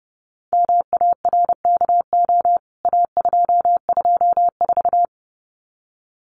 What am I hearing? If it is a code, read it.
Morse code, heard as a keyed tone, about 30 wpm, 700 Hz. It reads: MARKO A224
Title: CWT Exchange 30wpm